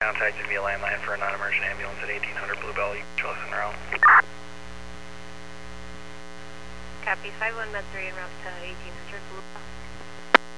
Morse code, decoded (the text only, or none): none